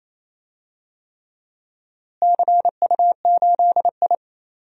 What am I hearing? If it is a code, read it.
Morse code, heard as a keyed tone, about 28 wpm, 700 Hz. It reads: CU8I
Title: DXCC-100 Call Signs - 4 Characters - Repeated 3x - 28wpm